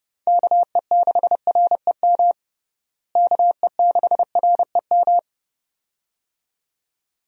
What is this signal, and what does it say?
Morse code, heard as a keyed tone, about 30 wpm, 700 Hz. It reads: KE6REM KE6REM
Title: USA Call Signs - 6 Characters - Repeated 2x - 30wpm